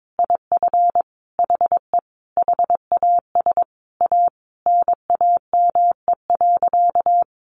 Morse code, heard as a keyed tone, about 22 wpm, 700 Hz. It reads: IF HE HAS A NAME.
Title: Sentences from Top 500 Words 22wpm